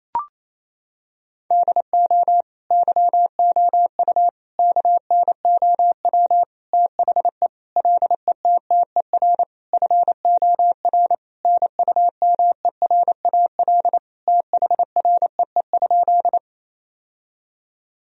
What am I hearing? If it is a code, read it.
Morse code, heard as a keyed tone, about 28 wpm, 700 Hz. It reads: DO YOU KNOW THE LETTER FOR NUMERAL THREE?
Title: Sentences from Top 1000 Words - Review Part 2 - 28wpm